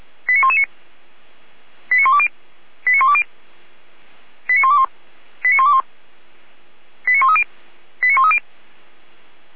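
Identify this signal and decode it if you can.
Five-tone selective call (ZVEI), heard as quick runs of five tones, tones of about 70 ms. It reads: 89119 89120 89120 89121 89121 89130 89130
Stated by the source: ZVEI five-tone selective call 89115 received on SDR.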